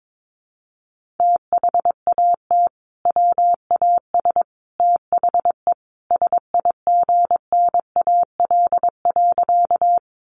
Morse code, heard as a keyed tone, about 22 wpm, 700 Hz. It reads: THAT WAS THE SIGNAL.
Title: Sentences from Top 1400 Words 22wpm